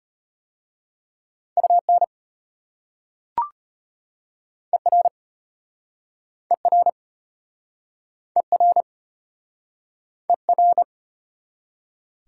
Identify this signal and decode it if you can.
Morse code, heard as a keyed tone, about 34 wpm, 700 Hz. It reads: UN ER ER ER ER